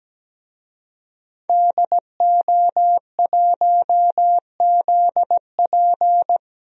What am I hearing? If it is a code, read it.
Morse code, heard as a keyed tone, about 17 wpm, 700 Hz. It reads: DO1ZP
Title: DXCC-100 Call Signs - 5 Characters - Repeated 3x - 17wpm